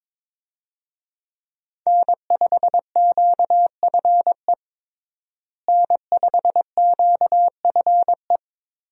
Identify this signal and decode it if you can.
Morse code, heard as a keyed tone, about 22 wpm, 700 Hz. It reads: N5QFE N5QFE
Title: USA Call Signs - 5 Characters - Repeated 3x - 22wpm